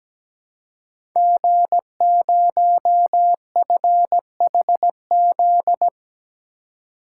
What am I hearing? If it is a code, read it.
Morse code, heard as a keyed tone, about 17 wpm, 700 Hz. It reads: G0FHZ
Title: Code Groups - 5 Letter:Numbers - 17wpm